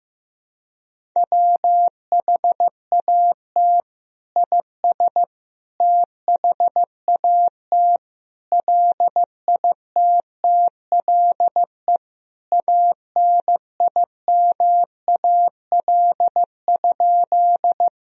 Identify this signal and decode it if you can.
Morse code, heard as a keyed tone, about 15 wpm, 700 Hz. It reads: WHAT IS THAT LITTLE ANIMAL?
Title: Sentences from Top 1000 Words - Review Part 1 - 15wpm